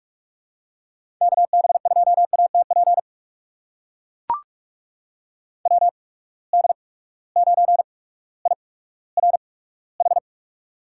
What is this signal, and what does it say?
Morse code, heard as a keyed tone, about 44 wpm, spaced as for 19 wpm, 700 Hz. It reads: KB2ATP WD9IRH
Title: USA Call Signs - 6 Characters - 8x Character Spacing - 45wpm